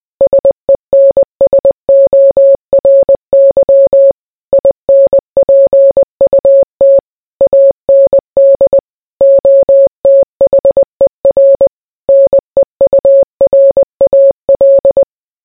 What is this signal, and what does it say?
Morse code, heard as a keyed tone, about 20 wpm, 550 Hz. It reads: SENSORY INPUT AND OTHER NEURAL